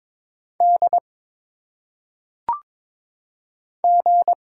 Morse code, heard as a keyed tone, about 22 wpm, 700 Hz. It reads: D G
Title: Single Letters - Phonetic Pronunciation 22wpm